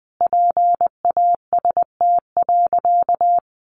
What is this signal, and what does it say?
Morse code, heard as a keyed tone, about 20 wpm, 700 Hz. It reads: PAST.
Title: A Sound Of Thunder 20wpm